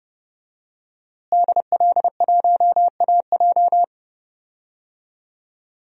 Morse code, heard as a keyed tone, about 30 wpm, 700 Hz. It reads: DL1AJ
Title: DXCC-100 Call Signs - 5 Characters - Repeated 3x - 30wpm